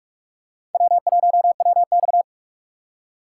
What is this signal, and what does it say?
Morse code, heard as a keyed tone, about 46 wpm, 700 Hz. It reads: W1WX
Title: USA Call Signs - 4 Characters - 4x Character Spacing - 45wpm